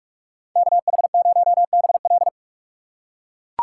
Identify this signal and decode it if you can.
Morse code, heard as a keyed tone, about 44 wpm, 700 Hz. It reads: KH0BL